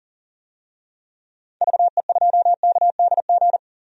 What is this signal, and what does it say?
Morse code, heard as a keyed tone, about 40 wpm, 700 Hz. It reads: VE2KDG
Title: DXCC-100 Call Signs - 6 Characters - 40wpm